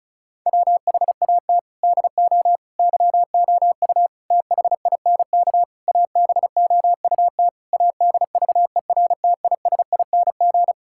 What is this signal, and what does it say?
Morse code, heard as a keyed tone, about 36 wpm, 700 Hz. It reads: WHAT DO YOU THINK ABOUT ADVERTISING
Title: Sentences from Top 1000 Words - Review Part 6 - 35wpm